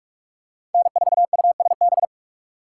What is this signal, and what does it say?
Morse code, heard as a keyed tone, about 41 wpm, 700 Hz. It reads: N4USB